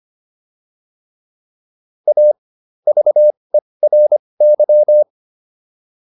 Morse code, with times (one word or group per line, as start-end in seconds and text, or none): 2.1-2.3 A
2.9-5.0 VERY